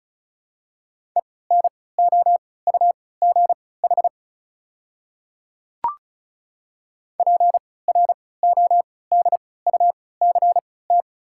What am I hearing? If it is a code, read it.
Morse code, heard as a keyed tone, about 36 wpm, spaced as for 20 wpm, 690 Hz. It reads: ENOUGH PRODUCT
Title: 3x Character Spacing - Top 500 Words - 35wpm